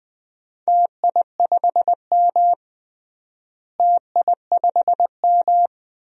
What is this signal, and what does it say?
Morse code, heard as a keyed tone, about 21 wpm, 700 Hz. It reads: TI5M TI5M
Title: DXCC-100 Call Signs - 4 Characters - Repeated 2x - 20wpm